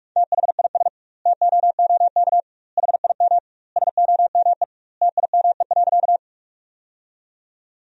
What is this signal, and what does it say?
Morse code, heard as a keyed tone, about 46 wpm, 700 Hz. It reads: THIS TOOK HIM SOME TIME.
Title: 2x Word-Spacing - Sentences from Top 300 Words - 45wpm